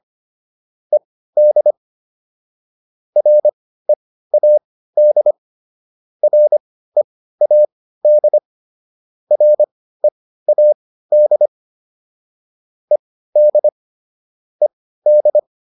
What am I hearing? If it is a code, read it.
Morse code, heard as a keyed tone, about 25 wpm, spaced as for 15 wpm, 600 Hz. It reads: ED READ READ READ ED ED